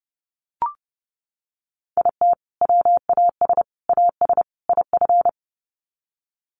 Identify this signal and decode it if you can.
Morse code, heard as a keyed tone, about 30 wpm, 700 Hz. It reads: IT WAS AS IF
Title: Sets of 4 Words 30wpm